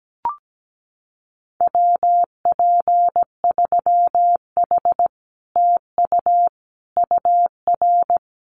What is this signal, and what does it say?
Morse code, heard as a keyed tone, about 17 wpm, 700 Hz. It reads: WP3H TU UR
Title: Common QSO Phrases Encore 17wpm